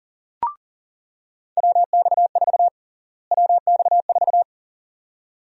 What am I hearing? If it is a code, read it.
Morse code, heard as a keyed tone, about 40 wpm, 700 Hz. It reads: WX4 WX4
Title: USA Call Sign Prefixes - Repeated 2x - 40wpm